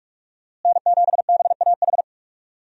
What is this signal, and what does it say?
Morse code, heard as a keyed tone, about 46 wpm, 700 Hz. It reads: N7BAH
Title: USA Call Signs - 5 Characters - Repeated 2x - 45wpm